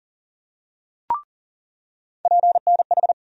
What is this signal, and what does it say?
Morse code, heard as a keyed tone, about 40 wpm, 700 Hz. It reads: PNH
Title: USA Call Sign Postfixes - 3 Letters 40wpm